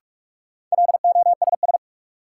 Morse code, heard as a keyed tone, about 46 wpm, 700 Hz. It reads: LOSS
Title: Sentences from Top 1400 Words - New Word Drill - 45wpm